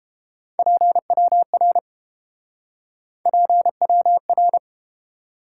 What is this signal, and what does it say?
Morse code, heard as a keyed tone, about 32 wpm, 700 Hz. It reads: PWR PWR